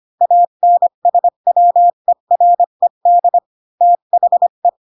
Morse code, heard as a keyed tone, about 25 wpm, 700 Hz. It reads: ANSWERED THE
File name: Pinocchio / Pinocchio ch4 20 25 ik1hge